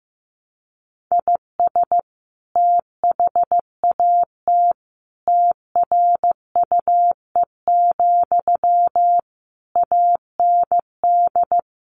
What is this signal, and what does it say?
Morse code, heard as a keyed tone, about 15 wpm, 700 Hz. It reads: IS THAT TRUE, AND
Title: Sentences from Top 1200 Words 15wpm